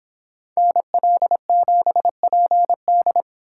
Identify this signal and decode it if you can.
Morse code, heard as a keyed tone, about 25 wpm, 700 Hz. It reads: NL7PD